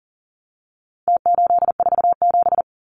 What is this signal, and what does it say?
Morse code, heard as a keyed tone, about 40 wpm, 700 Hz. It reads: T847
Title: Numbers - Contest Style - 4-Digit 40wpm